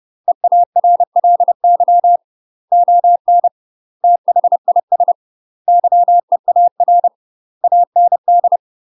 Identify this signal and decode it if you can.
Morse code, heard as a keyed tone, about 30 wpm, 700 Hz. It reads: EARLY ON THIS YEAR AND
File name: UK MASTER QSOs TRAINING / 30WPM / UK MASTER QSOs 0001